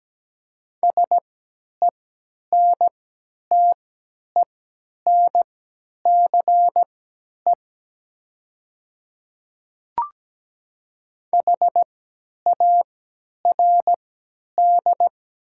Morse code, heard as a keyed tone, about 17 wpm, spaced as for 10 wpm, 690 Hz. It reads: SENTENCE HARD
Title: 3x Character Spacing - Top 300 Words - 17wpm